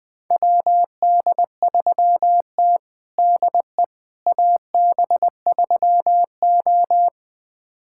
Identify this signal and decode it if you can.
Morse code, heard as a keyed tone, about 20 wpm, 700 Hz. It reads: WD3T DE AB3O
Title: USA Callsign DE Callsign - 4 Characters - 20wpm